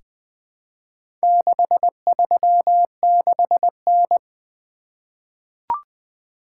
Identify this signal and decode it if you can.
Morse code, heard as a keyed tone, about 20 wpm, 700 Hz. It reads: 636N
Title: Four Letter-Number Combinations - 20wpm